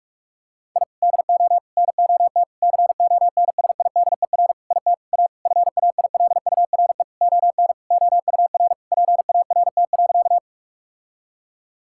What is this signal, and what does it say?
Morse code, heard as a keyed tone, about 41 wpm, 700 Hz. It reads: I DO NOT CONSIDER IT A FAILURE ON OUR PART.
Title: Sentences from Top 1000 Words - Review Part 6 - 45wpm